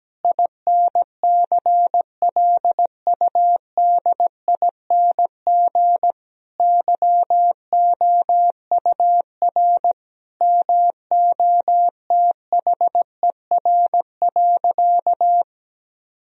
Morse code, heard as a keyed tone, about 17 wpm, 700 Hz. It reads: INCLUDING YOUR MOTHER.
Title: Sentences from Top 1000 Words - Review Part 4 - 17wpm